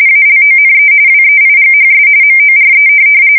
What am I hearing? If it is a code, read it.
RTTY, about 45.45 baud. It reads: ALONE WEIGHS MORE
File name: fsk170 / USS-New-Jersey